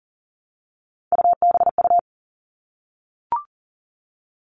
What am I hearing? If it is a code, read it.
Morse code, heard as a keyed tone, about 40 wpm, 700 Hz. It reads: UBU
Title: All Cut Numbers - 3-Digit 40wpm